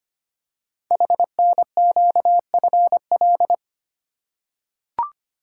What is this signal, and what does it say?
Morse code, heard as a keyed tone, about 25 wpm, 700 Hz. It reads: HNQFL
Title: Code Groups - 5 Letters - 25wpm